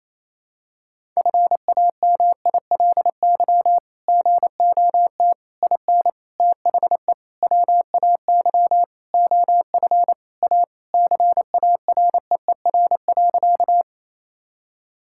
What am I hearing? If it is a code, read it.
Morse code, heard as a keyed tone, about 28 wpm, 700 Hz. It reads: FAMILY GOT IN THE WAY OF A CAREER.
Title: Sentences from Top 1000 Words - Review Part 6 - 28wpm